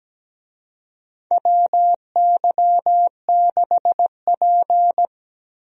Morse code, heard as a keyed tone, about 17 wpm, 700 Hz. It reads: WY6P
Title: USA Call Signs - 4 Characters - 17wpm